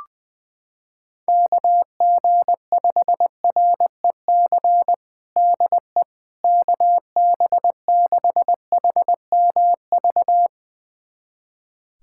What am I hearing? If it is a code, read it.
Morse code, heard as a keyed tone, about 20 wpm, 700 Hz. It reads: KG5REC DE KB6HMV
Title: USA Callsign DE Callsign - 6 Characters - 20wpm